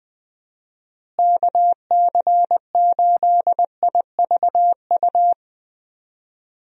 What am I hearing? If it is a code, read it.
Morse code, heard as a keyed tone, about 21 wpm, 700 Hz. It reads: KC8IVU
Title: USA Call Signs - 6 Characters - Repeated 3x - 20wpm